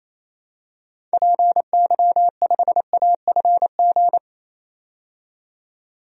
Morse code, heard as a keyed tone, about 28 wpm, 700 Hz. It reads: PY5AFG
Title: DXCC-100 Call Signs - 6 Characters - 28wpm